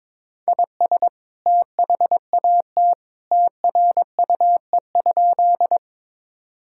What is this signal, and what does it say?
Morse code, heard as a keyed tone, about 22 wpm, 700 Hz. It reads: IS THAT TRUE?
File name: practice / 22wpm / Sentences with 2-Word Phrases - 22wpm